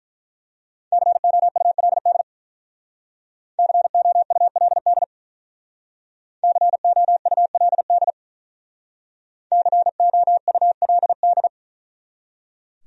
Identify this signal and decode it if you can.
Morse code, heard as a keyed tone, about 43 wpm, 700 Hz. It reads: COULD COULD COULD COULD
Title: Speed-Racing Top 100 Words 35wpm